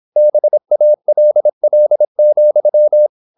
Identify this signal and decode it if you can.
Morse code, heard as a keyed tone, about 26 wpm, 600 Hz. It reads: BALL,